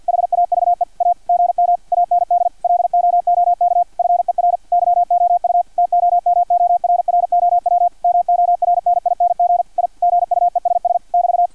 Morse code, heard as a keyed tone, about 50 wpm, 700 Hz, with radio noise. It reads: HAVE A GM AND LOOK FER YOU TOMORROW MORNING I GUESS <BT>
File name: archived / CRSnet